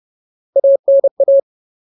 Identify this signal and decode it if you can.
Morse code, heard as a keyed tone, about 30 wpm, 550 Hz. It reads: ANA
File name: Ultimate Word List / 30WPM Ultimate